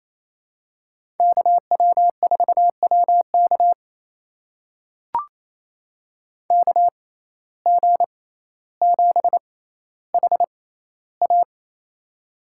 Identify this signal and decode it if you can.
Morse code, heard as a keyed tone, about 28 wpm, spaced as for 14 wpm, 700 Hz. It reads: KW4WK KG7HA
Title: USA Call Signs - 5 Characters - 6x Character Spacing - 28wpm